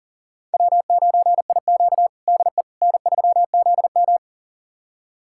Morse code, heard as a keyed tone, about 40 wpm, 700 Hz. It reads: W9IQ DE N3ZM
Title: USA Callsign DE Callsign - 4 Characters - 40wpm